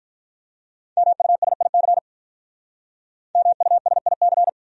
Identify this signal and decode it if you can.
Morse code, heard as a keyed tone, about 44 wpm, 700 Hz. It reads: MUSIC MUSIC